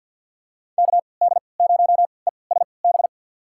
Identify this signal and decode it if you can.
Morse code, heard as a keyed tone, about 52 wpm, spaced as for 29 wpm, 700 Hz. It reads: KD0ESB
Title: USA Call Signs - 6 Characters - 3x Character Spacing - 50wpm